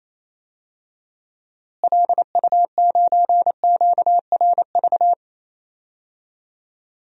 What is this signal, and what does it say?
Morse code, heard as a keyed tone, about 28 wpm, 700 Hz. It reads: LU9QRV
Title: DXCC-100 Call Signs - 6 Characters - Repeated 2x - 28wpm